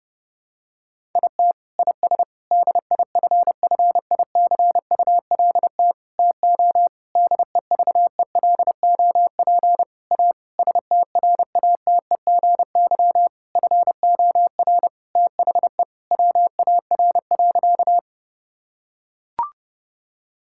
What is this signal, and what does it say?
Morse code, heard as a keyed tone, about 30 wpm, 700 Hz. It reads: IT IS DIFFICULT TO DEVELOP A STRATEGY FOR THE WAR.
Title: Sentences from Top 1000 Words - Review Part 5 - 30wpm